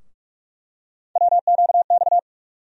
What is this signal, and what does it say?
Morse code, heard as a keyed tone, about 43 wpm, 700 Hz. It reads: WQX
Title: USA Call Sign Postfixes - 3 Letters 45wpm